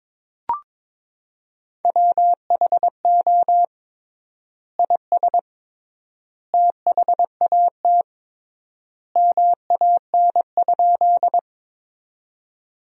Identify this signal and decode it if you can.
Morse code, heard as a keyed tone, about 21 wpm, 700 Hz. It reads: WHO IS THAT MAN?
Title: 3x Word-Spacing - Sentences from Top 200 Words - 22wpm